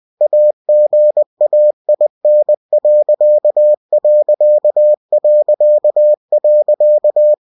Morse code, heard as a keyed tone, about 20 wpm, 600 Hz. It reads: AGAIN....
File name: cw-books / winnie-the-pooh / winnie-20wpm-page-0101